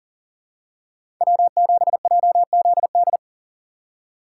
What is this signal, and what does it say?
Morse code, heard as a keyed tone, about 40 wpm, 700 Hz. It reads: W7JZD
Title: USA Call Signs - 5 Characters - 6x Character Spacing - 40wpm